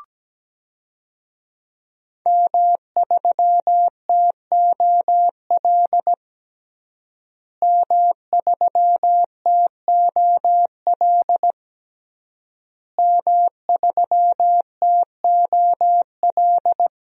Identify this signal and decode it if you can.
Morse code, heard as a keyed tone, about 17 wpm, 700 Hz. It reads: M3TOL M3TOL M3TOL
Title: DXCC-100 Call Signs - 5 Characters - Repeated 3x - 17wpm